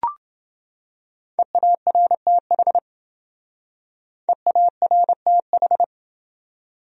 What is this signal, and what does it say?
Morse code, heard as a keyed tone, about 27 wpm, 700 Hz. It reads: EARTH EARTH